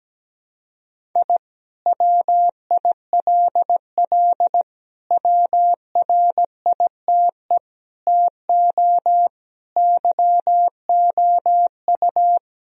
Morse code, heard as a keyed tone, about 17 wpm, 700 Hz. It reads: I WILL WRITE TO YOU
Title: Sentences from Top 1000 Words - Review Part 1 - 17wpm